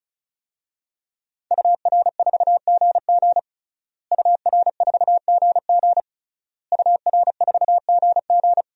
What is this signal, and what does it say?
Morse code, heard as a keyed tone, about 34 wpm, 700 Hz. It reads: UR4GG UR4GG UR4GG
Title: DXCC-100 Call Signs - 5 Characters - Repeated 3x - 35wpm